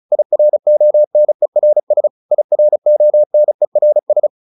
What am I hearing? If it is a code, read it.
Morse code, heard as a keyed tone, about 34 wpm, 600 Hz. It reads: IRONERS IRONERS